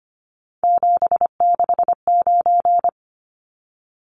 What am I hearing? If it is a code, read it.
Morse code, heard as a keyed tone, about 25 wpm, 700 Hz. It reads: 769
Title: Numbers - 3-Digit 25wpm